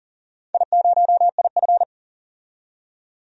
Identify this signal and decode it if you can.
Morse code, heard as a keyed tone, about 40 wpm, 700 Hz. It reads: I0IF
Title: DXCC-100 Call Signs - 4 Characters - 40wpm